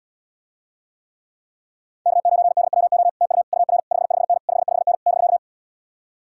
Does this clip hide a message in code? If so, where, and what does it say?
Yes: Morse code from 2.1 to 5.4 s, heard as a keyed tone, about 77 wpm, 700 Hz. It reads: K9DWO TU UR 55N 55N <BK>